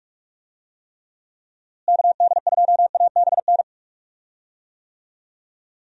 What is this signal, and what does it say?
Morse code, heard as a keyed tone, about 46 wpm, 700 Hz. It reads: KD2ABN
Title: USA Call Signs - 6 Characters - Repeated 2x - 45wpm